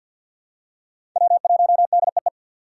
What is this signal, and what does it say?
Morse code, heard as a keyed tone, about 52 wpm, 700 Hz. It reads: W1DEE
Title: USA Call Signs - 2x Character Spacing - 50wpm